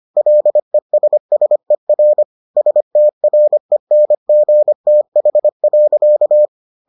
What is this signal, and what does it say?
Morse code, heard as a keyed tone, about 25 wpm, 600 Hz. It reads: LESSER STRENGTH.